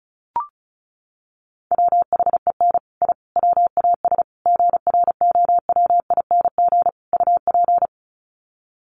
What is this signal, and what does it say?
Morse code, heard as a keyed tone, about 34 wpm, 700 Hz. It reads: WHEN I WAS GROWING UP
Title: Sets of 5 Words Encore 35wpm